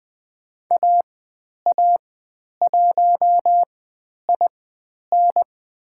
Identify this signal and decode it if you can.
Morse code, heard as a keyed tone, about 20 wpm, spaced as for 10 wpm, 700 Hz. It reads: AA1IN
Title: 095 Lesson 16g - Letter c - Call Signs 20wpm